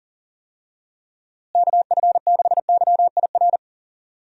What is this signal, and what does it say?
Morse code, heard as a keyed tone, about 40 wpm, 700 Hz. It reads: KF6YIR
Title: USA Call Signs - 6 Characters - Repeated 2x - 40wpm